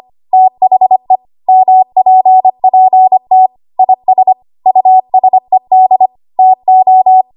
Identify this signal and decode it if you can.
Morse code, heard as a keyed tone, about 25 wpm, 750 Hz. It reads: THE MPPT IS USED TO
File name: Archive / 25 WPM / 25WPM